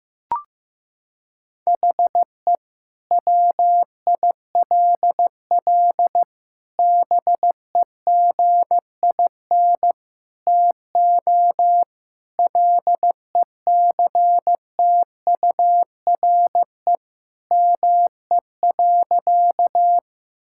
Morse code, heard as a keyed tone, about 15 wpm, 700 Hz. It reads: HE WILL BEGIN TO LECTURE ME.
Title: Sentences from Top 1600 Words 15wpm